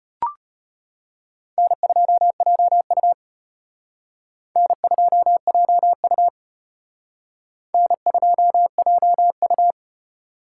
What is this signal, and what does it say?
Morse code, heard as a keyed tone, about 31 wpm, 700 Hz. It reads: N2JU N2JU N2JU
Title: Speed Racing - USA 1x2 Call Signs 25wpm